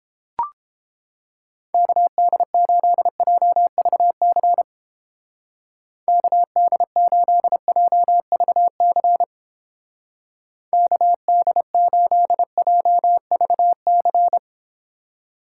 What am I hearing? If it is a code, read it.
Morse code, heard as a keyed tone, about 31 wpm, 700 Hz. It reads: KD8JVC KD8JVC KD8JVC